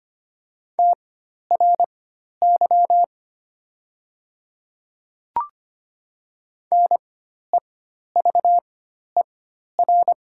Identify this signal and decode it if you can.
Morse code, heard as a keyed tone, about 25 wpm, spaced as for 12 wpm, 700 Hz. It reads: TRY NEVER
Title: 4x Character Spacing - Top 500 Words - 25wpm